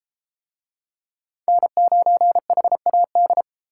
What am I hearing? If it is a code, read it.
Morse code, heard as a keyed tone, about 32 wpm, 700 Hz. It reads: N9HAD